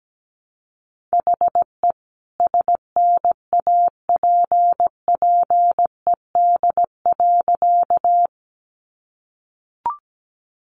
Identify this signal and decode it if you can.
Morse code, heard as a keyed tone, about 17 wpm, 700 Hz. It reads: HE SNAPPED.
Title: A Sound Of Thunder 17wpm